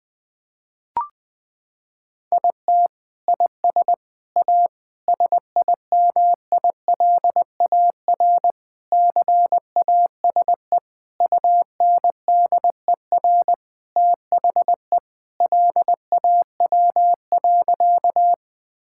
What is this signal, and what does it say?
Morse code, heard as a keyed tone, about 19 wpm, 700 Hz. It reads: IT IS A SIMILAR CASE UNDER THE LAW.
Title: Sentences from Top 1000 Words - Review Part 5 - 20wpm